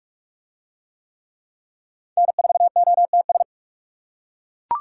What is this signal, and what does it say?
Morse code, heard as a keyed tone, about 39 wpm, 700 Hz. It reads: N4OTS